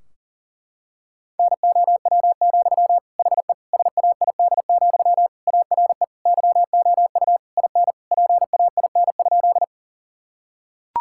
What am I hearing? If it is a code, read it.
Morse code, heard as a keyed tone, about 40 wpm, 700 Hz. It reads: NOW, HE SAID, ARE YOU IN PAIN?
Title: Sentences from Top 1600 Words 40wpm